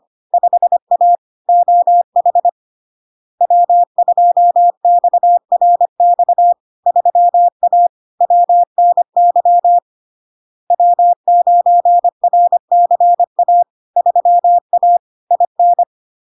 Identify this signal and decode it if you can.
Morse code, heard as a keyed tone, about 25 wpm, 700 Hz. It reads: 5A OH W2XRX 3A WNY W9RCA 3A IN
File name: Log-in-CW-25wpm-700-Hz